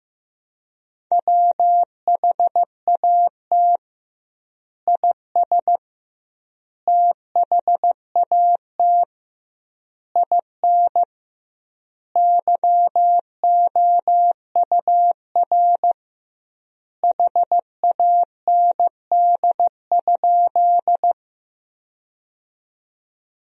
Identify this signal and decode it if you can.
Morse code, heard as a keyed tone, about 15 wpm, 700 Hz. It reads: WHAT IS THAT IN YOUR HAND?
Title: 2x Word-Spacing - Sentences from Top 200 Words - 15wpm